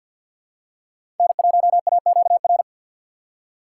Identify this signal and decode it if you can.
Morse code, heard as a keyed tone, about 52 wpm, 700 Hz. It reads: N1AQR
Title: USA Call Signs - 5 Characters - 6x Character Spacing - 50wpm